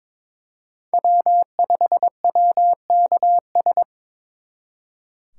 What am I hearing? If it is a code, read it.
Morse code, heard as a keyed tone, about 27 wpm, 700 Hz. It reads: W5WKS
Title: Speed Racing - USA 1x3 Call Signs 22wpm